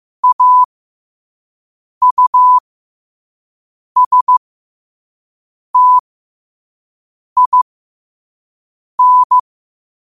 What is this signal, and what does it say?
Morse code, heard as a keyed tone, about 15 wpm, spaced as for 5 wpm, 1000 Hz. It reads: AUSTIN
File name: qotdfiles / 5wpm